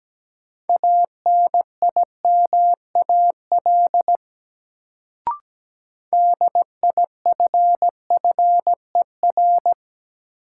Morse code, heard as a keyed tone, about 17 wpm, 700 Hz. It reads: ANIMAL DIFFER
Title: Sentences from Top 1000 Words - Review Part 1 - New Word Drill - 17wpm